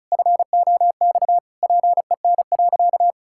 Morse code, heard as a keyed tone, about 36 wpm, 700 Hz. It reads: FOX PEN.
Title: Pangram Challenge v2 - 35wpm